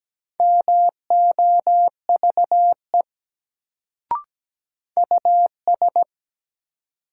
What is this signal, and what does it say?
Morse code, heard as a keyed tone, about 17 wpm, 700 Hz. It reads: MOVE US
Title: Sentences from Top 200 Words - New Word Drill - 17wpm